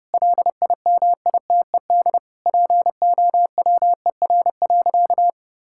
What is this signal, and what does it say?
Morse code, heard as a keyed tone, about 30 wpm, 700 Hz. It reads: LIMITED POWER.
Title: Sentences from Top 1000 Words - Review Part 6 - 30wpm